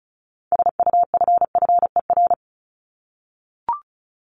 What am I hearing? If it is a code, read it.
Morse code, heard as a keyed tone, about 34 wpm, 700 Hz. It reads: SUFFER
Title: Top 1501-1750 Words 35wpm